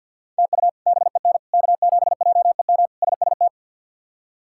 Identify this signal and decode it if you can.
Morse code, heard as a keyed tone, about 50 wpm, 700 Hz. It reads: TU BEN K7JEM SST
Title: K1USN SST QSO Practice - 50wpm